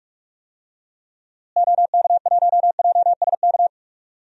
Morse code, heard as a keyed tone, about 46 wpm, 700 Hz. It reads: OK1JSK
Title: DXCC-100 Call Signs - 6 Characters - Repeated 2x - 45wpm